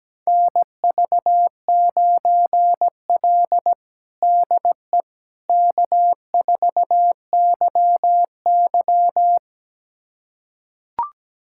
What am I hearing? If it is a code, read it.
Morse code, heard as a keyed tone, about 17 wpm, 700 Hz. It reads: NV9L DE K4YY
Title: USA Callsign DE Callsign - 4 Characters - 17wpm